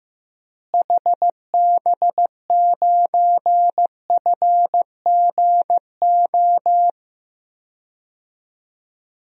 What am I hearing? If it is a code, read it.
Morse code, heard as a keyed tone, about 15 wpm, 700 Hz. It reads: HB9FGO
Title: DXCC-100 Call Signs - 6 Characters - Repeated 2x - 15wpm